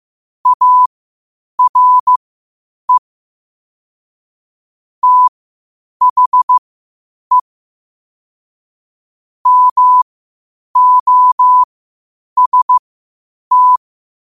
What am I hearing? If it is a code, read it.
Morse code, heard as a keyed tone, about 15 wpm, spaced as for 8 wpm, 1000 Hz. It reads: ARE THE MOST
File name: qotdfiles / 8wpm